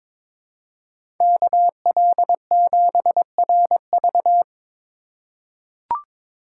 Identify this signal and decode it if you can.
Morse code, heard as a keyed tone, about 22 wpm, 700 Hz. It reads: KL7RV
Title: USA Call Signs - 5 Characters - 3x Character Spacing - 22wpm